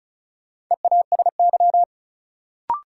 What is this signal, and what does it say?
Morse code, heard as a keyed tone, about 36 wpm, 700 Hz. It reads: EASY
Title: Sentences from Top 700 Words - New Word Drill - 35wpm